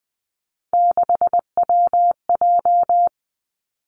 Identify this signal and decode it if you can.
Morse code, heard as a keyed tone, about 20 wpm, 700 Hz. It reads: 6WJ
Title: Three Challenging Letter-Number Combinations 20wpm